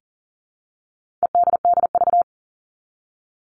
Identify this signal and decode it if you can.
Morse code, heard as a keyed tone, about 40 wpm, 700 Hz. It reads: EDDV
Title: All Cut Numbers - 4-Digit 40wpm